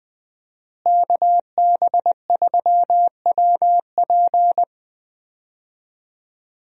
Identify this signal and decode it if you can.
Morse code, heard as a keyed tone, about 20 wpm, 700 Hz. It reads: KB3WP
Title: USA Call Signs - 5 Characters - Repeated 2x - 20wpm